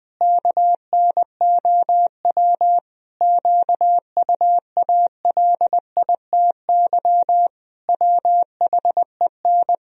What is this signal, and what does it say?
Morse code, heard as a keyed tone, about 20 wpm, 700 Hz. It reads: KNOW QUALITY WHEN
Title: Sentences from Top 1000 Words - Review Part 4 - 20wpm